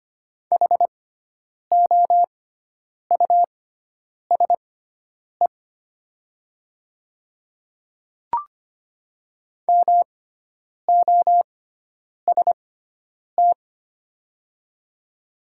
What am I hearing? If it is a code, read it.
Morse code, heard as a keyed tone, about 25 wpm, spaced as for 9 wpm, 700 Hz. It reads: HOUSE MOST
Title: 6x Character Spacing - Top 200 Words - 25wpm